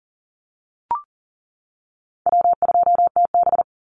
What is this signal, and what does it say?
Morse code, heard as a keyed tone, about 40 wpm, 700 Hz. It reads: W2TB
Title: Easy Call Signs - 40wpm